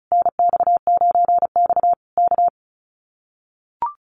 Morse code, heard as a keyed tone, about 34 wpm, 700 Hz. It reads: NX9X K
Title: Common QSO Phrases Encore 35wpm